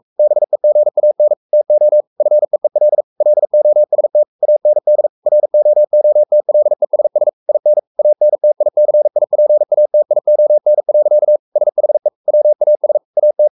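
Morse code, heard as a keyed tone, about 44 wpm, 600 Hz. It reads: BEGAN TO FEEL LOST AND ROOTLESS IN ANTICIPATION. SHE WAS AT
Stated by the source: Morse code at 43 wpm sending Ken Follett's novel Never in English, from CW med Gnister.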